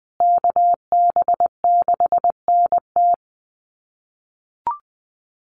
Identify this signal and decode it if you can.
Morse code, heard as a keyed tone, about 20 wpm, 700 Hz. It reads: KB6NT
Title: USA Call Signs Repeated 3x 20wpm